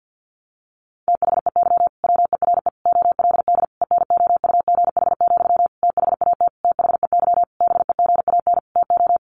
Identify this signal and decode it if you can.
Morse code, heard as a keyed tone, about 50 wpm, 700 Hz. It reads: THEY WERE OLD ENOUGH, THAT THEY BEGAN TO
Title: Sentences from Top 800 Words 50wpm